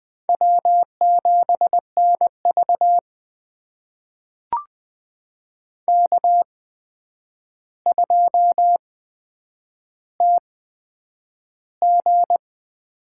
Morse code, heard as a keyed tone, about 21 wpm, spaced as for 9 wpm, 700 Hz. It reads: W7NV K2TG
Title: USA Call Signs - 4 Characters - 8x Character Spacing - 20wpm